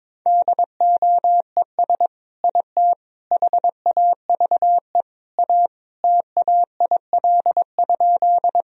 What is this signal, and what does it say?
Morse code, heard as a keyed tone, about 22 wpm, 700 Hz. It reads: DOES IT HAVE A TAIL?
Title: Sentences from Top 1000 Words - Review Part 3 - 22wpm